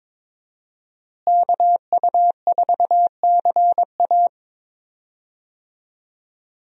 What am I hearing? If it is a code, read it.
Morse code, heard as a keyed tone, about 22 wpm, 700 Hz. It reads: KU4CA
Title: USA Call Signs - 5 Characters - Repeated 3x - 22wpm